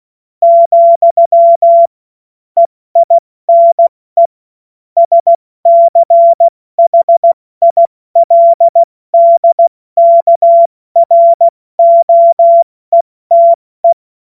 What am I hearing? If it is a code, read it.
Morse code, heard as a keyed tone, about 16 wpm, 680 Hz. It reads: , EINE SCHILDKROETE
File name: Morsekurs / Witze / Witz 10. August 2021, 70 F80